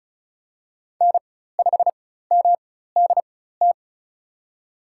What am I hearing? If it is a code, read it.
Morse code, heard as a keyed tone, about 34 wpm, spaced as for 16 wpm, 700 Hz. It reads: N5MDT
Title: USA Call Signs - 5 Characters - 4x Character Spacing - 35wpm